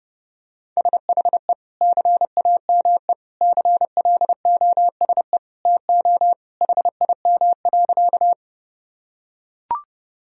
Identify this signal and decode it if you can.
Morse code, heard as a keyed tone, about 32 wpm, 700 Hz. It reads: SHE CAME CLOSE TO HIM.
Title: Sentences from Top 1000 Words - Review Part 1 - 30wpm